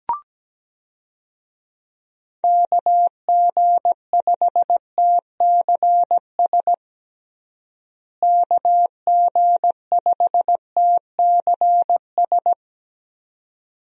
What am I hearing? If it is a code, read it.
Morse code, heard as a keyed tone, about 17 wpm, 700 Hz. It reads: KG5TCS KG5TCS
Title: USA Call Signs - 6 Characters - Repeated 2x - 17wpm